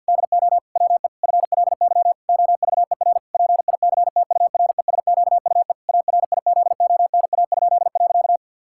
Morse code, heard as a keyed tone, about 52 wpm, 700 Hz. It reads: DO WE FLY OVER PICTURESQUE ARIZONA?.
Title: Pangram Challenge v2 - 50wpm